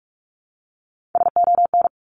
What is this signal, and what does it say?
Morse code, heard as a keyed tone, about 44 wpm, 700 Hz. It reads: SON
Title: Top 1751-2000 Words 45wpm